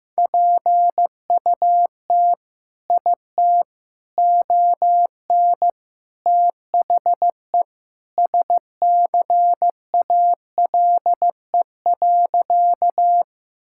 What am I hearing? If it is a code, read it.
Morse code, heard as a keyed tone, about 16 wpm, 700 Hz. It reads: PUT IT ON THE SCALE.
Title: Sentences from Top 1000 Words - Review Part 5 - 15wpm